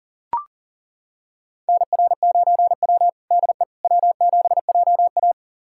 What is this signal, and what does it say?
Morse code, heard as a keyed tone, about 37 wpm, 700 Hz. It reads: NR9W DE W7JA
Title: USA Callsign DE Callsign - 4 Characters - 40wpm